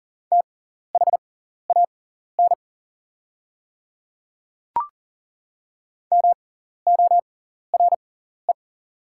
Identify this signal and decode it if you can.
Morse code, heard as a keyed tone, about 40 wpm, spaced as for 14 wpm, 690 Hz. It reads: THAN MORE
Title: 6x Character Spacing - Top 100 Words - 40wpm